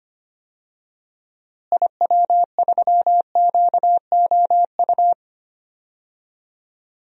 Morse code, heard as a keyed tone, about 25 wpm, 700 Hz. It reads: IW3QOU
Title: DXCC-100 Call Signs - 6 Characters - Repeated 3x - 25wpm